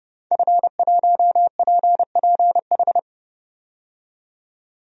Morse code, heard as a keyed tone, about 32 wpm, 700 Hz. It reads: F1PPH
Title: DXCC-100 Call Signs - 5 Characters - 30wpm